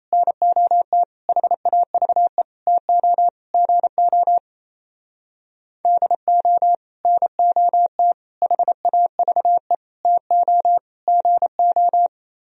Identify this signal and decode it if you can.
Morse code, heard as a keyed tone, about 34 wpm, 700 Hz. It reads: NOT HAVE TO GO DO NOT HAVE TO GO
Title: Speed-Racing - Sets of 5 Words 28wpm